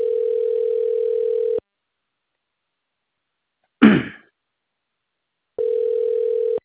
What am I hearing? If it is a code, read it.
Ringback tone (North America), 440 + 480 Hz.
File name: recordings / REa805a90f149fd0f6de922a fd